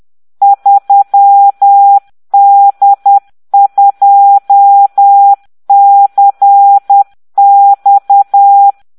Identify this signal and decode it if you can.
Morse code, heard as a keyed tone, about 10 wpm, 800 Hz. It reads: 3D2CX